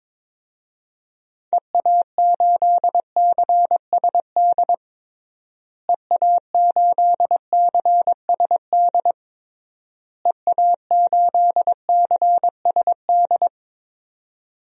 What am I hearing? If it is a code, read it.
Morse code, heard as a keyed tone, about 22 wpm, 700 Hz. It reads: EA8CSD EA8CSD EA8CSD
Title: DXCC-100 Call Signs - 6 Characters - Repeated 3x - 22wpm